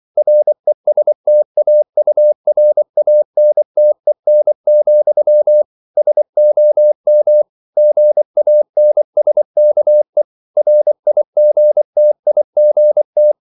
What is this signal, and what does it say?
Morse code, heard as a keyed tone, about 24 wpm, 600 Hz. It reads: RESTAURANTEN, SOM GANSKE RIGTIGT
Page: Ferie 24 wpm